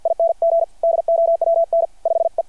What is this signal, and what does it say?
Morse code, heard as a keyed tone, about 53 wpm, 660 Hz. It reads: ING DOWN 5E
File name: archived / CRSnet